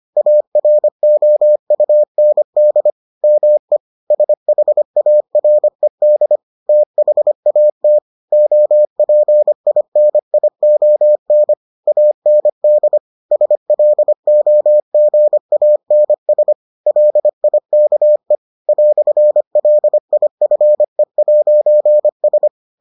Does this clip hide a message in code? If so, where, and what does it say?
Yes: Morse code from 0.2 to 22.5 s, heard as a keyed tone, about 25 wpm, 600 Hz. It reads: AROUND ME SHARED THAT OPINION AND SLOGANS LIKE "LIFE'S